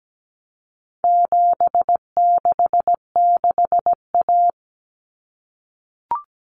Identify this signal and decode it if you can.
Morse code, heard as a keyed tone, about 17 wpm, 700 Hz. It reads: 766A
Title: Common Cut Numbers - 4-Digit 17wpm